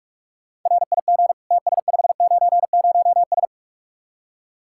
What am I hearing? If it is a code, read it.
Morse code, heard as a keyed tone, about 56 wpm, 700 Hz. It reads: RIG TS590S
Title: Speed-Racing Common QSO Phrases - 45wpm